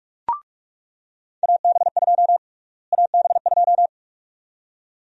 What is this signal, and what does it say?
Morse code, heard as a keyed tone, about 42 wpm, 700 Hz. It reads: AB2 AB2
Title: USA Call Sign Prefixes - Repeated 2x - 45wpm